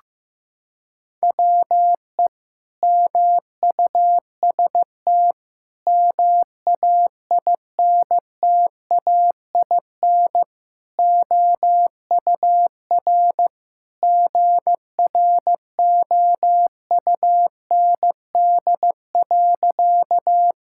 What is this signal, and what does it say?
Morse code, heard as a keyed tone, about 15 wpm, 700 Hz. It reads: WE MUST MAINTAIN OUR GROUND.
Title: Sentences from Top 1000 Words - Review Part 5 - 15wpm